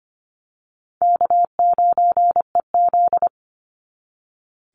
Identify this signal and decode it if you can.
Morse code, heard as a keyed tone, about 25 wpm, 700 Hz. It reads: K9EZ
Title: NA SOTA Chaser Callsigns 25wpm